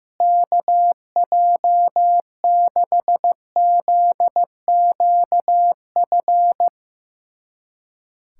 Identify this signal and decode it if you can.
Morse code, heard as a keyed tone, about 15 wpm, 700 Hz. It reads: KJ6ZQF
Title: USA Callsign DE Callsign 15wpm